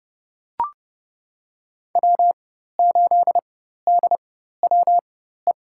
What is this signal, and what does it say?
Morse code, heard as a keyed tone, about 32 wpm, spaced as for 14 wpm, 700 Hz. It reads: W8DWE
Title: USA Call Signs - 4x Character Spacing - 30wpm